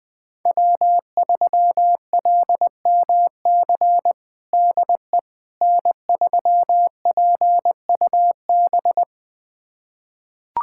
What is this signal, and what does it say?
Morse code, heard as a keyed tone, about 20 wpm, 700 Hz. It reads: W3LMC DE N3PUB
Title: USA Callsign DE Callsign - 5 Characters - 20wpm